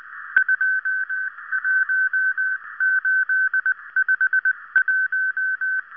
Morse code, heard as a keyed tone, about 19 wpm, 1550 Hz. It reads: 21851